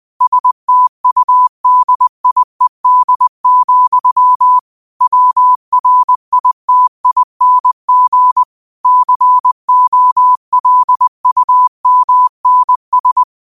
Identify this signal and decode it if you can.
Morse code, heard as a keyed tone, about 20 wpm, 1000 Hz. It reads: STUDIED, WRITING COLUMNS
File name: qotdfiles / 20wpm